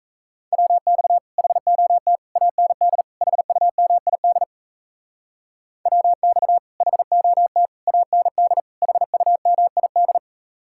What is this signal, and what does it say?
Morse code, heard as a keyed tone, about 37 wpm, 700 Hz. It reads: WX HOT AND HUMID WX HOT AND HUMID